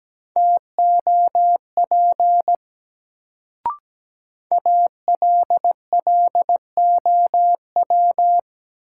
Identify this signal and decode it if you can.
Morse code, heard as a keyed tone, about 17 wpm, 700 Hz. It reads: TOP ALLOW
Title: Sentences from Top 700 Words - New Word Drill - 17wpm